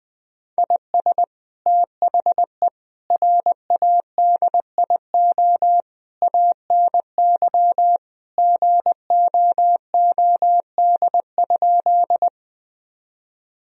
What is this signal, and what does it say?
Morse code, heard as a keyed tone, about 21 wpm, 700 Hz. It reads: IS THE RADIO ANY GOOD?
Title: Sentences from Top 1000 Words - Review Part 3 - 20wpm